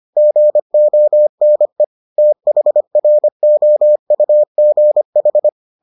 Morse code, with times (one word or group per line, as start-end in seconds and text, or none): 0.2-1.8 GONE
2.2-5.5 THROUGH